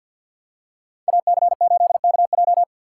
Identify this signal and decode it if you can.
Morse code, heard as a keyed tone, about 53 wpm, 690 Hz. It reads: AC8KJ
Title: USA Call Signs - 5 Characters - 6x Character Spacing - 50wpm